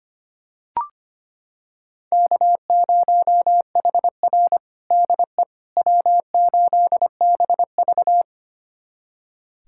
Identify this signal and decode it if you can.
Morse code, heard as a keyed tone, about 25 wpm, 700 Hz. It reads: K0HR DE W8BV
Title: USA Callsign DE Callsign - 4 Characters - 25wpm